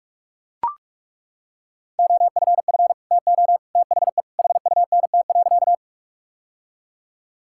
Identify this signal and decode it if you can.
Morse code, heard as a keyed tone, about 41 wpm, 700 Hz. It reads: OFF TO THE HUNT.
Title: Sentences from Top 1600 Words 45wpm